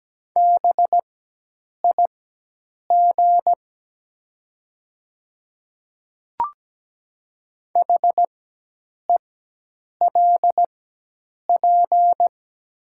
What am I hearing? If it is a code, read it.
Morse code, heard as a keyed tone, about 17 wpm, spaced as for 8 wpm, 690 Hz. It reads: BIG HELP
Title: 4x Character Spacing - Top 300 Words - 17wpm